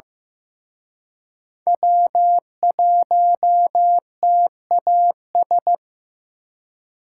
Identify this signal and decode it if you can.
Morse code, heard as a keyed tone, about 18 wpm, 700 Hz. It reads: W1TAS
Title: Speed Racing - USA 1x3 Call Signs 15wpm